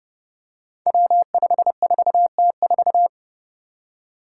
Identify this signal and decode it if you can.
Morse code, heard as a keyed tone, about 28 wpm, 700 Hz. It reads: W54T4
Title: Five Challenging Letter-Number Combinations 30wpm